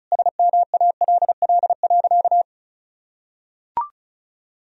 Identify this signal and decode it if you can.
Morse code, heard as a keyed tone, about 36 wpm, 700 Hz. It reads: SMALL.
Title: Sentences from Top 1000 Words - Review Part 4 - 35wpm